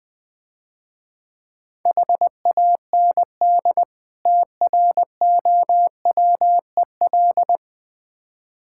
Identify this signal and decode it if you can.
Morse code, heard as a keyed tone, about 20 wpm, 690 Hz. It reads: HAND TROWEL
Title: Difficult Words - Speed-Racing with Hints - 20wpm